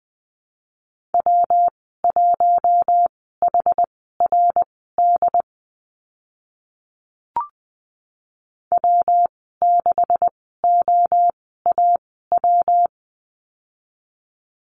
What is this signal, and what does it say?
Morse code, heard as a keyed tone, about 20 wpm, spaced as for 14 wpm, 700 Hz. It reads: W1HRD W6OAW
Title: 167 Lesson 27g - Number 7 - Call Signs 20wpm